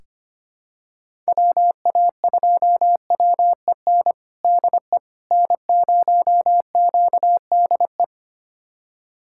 Morse code, heard as a keyed tone, about 25 wpm, 700 Hz. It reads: WA2WEN DE N0QDE
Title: USA Callsign DE Callsign 25wpm